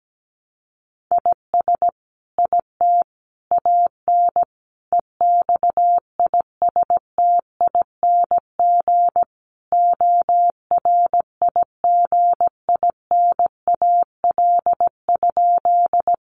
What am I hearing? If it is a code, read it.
Morse code, heard as a keyed tone, about 17 wpm, 700 Hz. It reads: IS IT AN EXISTING ORIGINAL?
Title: Sentences from Top 1500 Words 17wpm